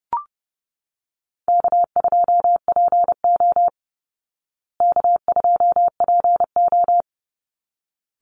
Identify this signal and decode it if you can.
Morse code, heard as a keyed tone, about 30 wpm, 700 Hz. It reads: K2PO K2PO
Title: Easy Call Signs Repeated 2x - 30wpm